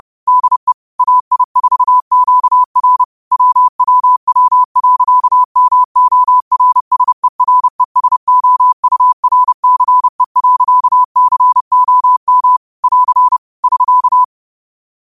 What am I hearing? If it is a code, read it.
Morse code, heard as a keyed tone, about 30 wpm, 1000 Hz. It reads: DE AI4QR WWW.MORSERESOURCE.COM <AR> <SK>